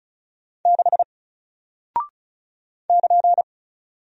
Morse code, heard as a keyed tone, about 34 wpm, 700 Hz. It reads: 6 <KN>
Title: Single Character (Letter:Number:Punctuation:Prosign) Alt 35wpm